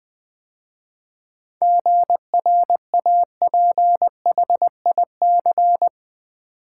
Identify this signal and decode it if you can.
Morse code, heard as a keyed tone, about 20 wpm, 700 Hz. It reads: GRAPHIC
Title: 7 Letter Words - First 3 Letters Given - 20wpm